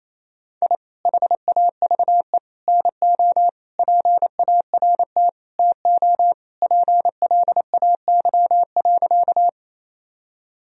Morse code, heard as a keyed tone, about 28 wpm, 700 Hz. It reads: I HAVE NO PART TO PLAY.
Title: Sentences from Top 1000 Words - Review Part 1 - 28wpm